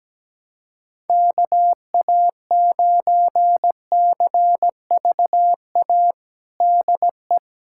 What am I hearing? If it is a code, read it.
Morse code, heard as a keyed tone, about 17 wpm, 700 Hz. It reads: KA9CVA DE
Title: USA Callsign DE Callsign 17wpm